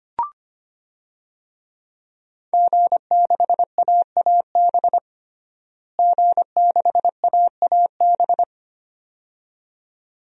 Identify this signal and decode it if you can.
Morse code, heard as a keyed tone, about 25 wpm, 700 Hz. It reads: G6AAB G6AAB
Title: DXCC-100 Call Signs - 5 Characters - Repeated 2x - 25wpm